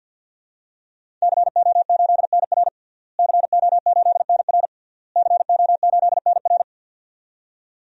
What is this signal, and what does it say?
Morse code, heard as a keyed tone, about 46 wpm, 700 Hz. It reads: CO8NR CO8NR CO8NR
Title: DXCC-100 Call Signs - 5 Characters - Repeated 3x - 50wpm